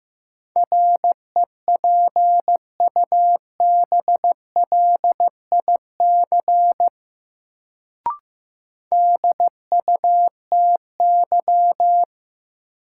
Morse code, heard as a keyed tone, about 15 wpm, 700 Hz. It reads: REPUBLIC DUTY
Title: Sentences from Top 1500 Words - New Word Drill - 15wpm